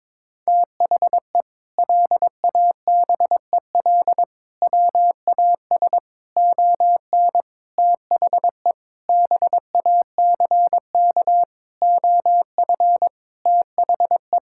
Morse code, heard as a keyed tone, about 22 wpm, 700 Hz. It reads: THE LABEL WAS ON THE BACK OF THE
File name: practice / 22wpm / Sentences with 4-Word Phrases - 22wpm